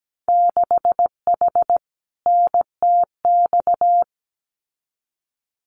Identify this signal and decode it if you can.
Morse code, heard as a keyed tone, about 17 wpm, 700 Hz. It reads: 6H NTX
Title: Winter Field Day Exchanges Encore 17wpm